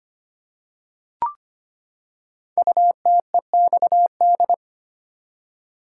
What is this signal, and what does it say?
Morse code, heard as a keyed tone, about 25 wpm, 700 Hz. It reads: UTEXD
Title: Code Groups - 5 Letters - 25wpm